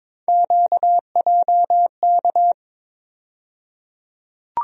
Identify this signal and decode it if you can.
Morse code, heard as a keyed tone, about 22 wpm, 700 Hz. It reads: QJK
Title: USA Call Sign Postfixes - 3 Letters 22wpm